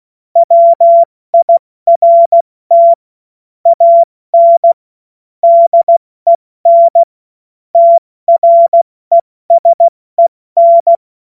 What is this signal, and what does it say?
Morse code, heard as a keyed tone, about 16 wpm, 680 Hz. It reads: WIRT AN DEN TRESEN
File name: Morsekurs / Witze / Witz 19. Oktober 2021, 70 F80